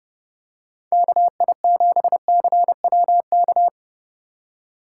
Morse code, heard as a keyed tone, about 29 wpm, 700 Hz. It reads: KI7CWK